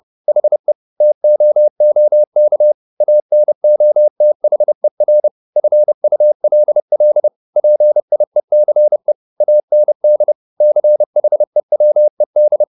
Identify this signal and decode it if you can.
Morse code, heard as a keyed tone, about 30 wpm, 600 Hz. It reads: HE TOOK ANOTHER FULL PIECE AND CHEWED
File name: the old man and the sea / the old man and the sea 1 30wpm